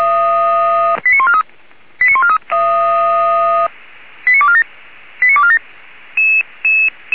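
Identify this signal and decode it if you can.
Five-tone selective call (ZVEI), heard as quick runs of five tones, tones of about 70 ms. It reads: 89143 89143 89247 89247
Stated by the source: ZVEI five-tone selective call received by SDR, Feuerwehr Orlamünde audio server.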